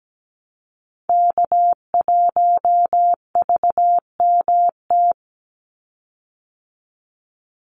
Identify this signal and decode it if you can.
Morse code, heard as a keyed tone, about 17 wpm, 700 Hz. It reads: K1VMT
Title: Call Signs Repeated 2x - 17wpm